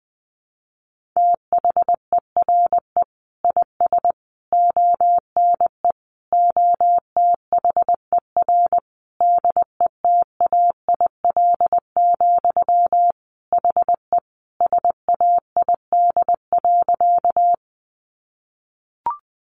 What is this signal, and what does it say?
Morse code, heard as a keyed tone, about 20 wpm, 700 Hz. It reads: THERE IS ONE OTHER DETAIL, HE SAID.
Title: Sentences from Top 1500 Words 20wpm